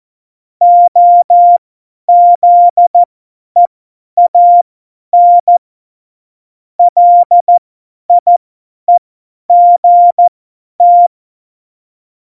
Morse code, heard as a keyed tone, about 14 wpm, spaced as for 10 wpm, 700 Hz. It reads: OZEAN LIEGT
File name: Morsekurs / Witze / Witz 04. Juni 2019, 50 F70